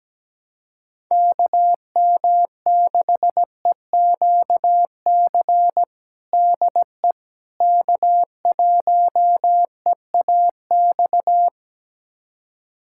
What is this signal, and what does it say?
Morse code, heard as a keyed tone, about 17 wpm, 700 Hz. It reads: KM6EQC DE K1EAX
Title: USA Callsign DE Callsign 17wpm